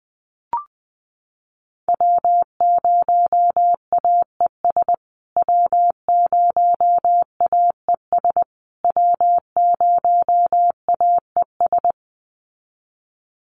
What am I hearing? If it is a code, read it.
Morse code, heard as a keyed tone, about 20 wpm, 700 Hz. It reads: W0AES W0AES W0AES
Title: USA Call Signs Repeated 3x 20wpm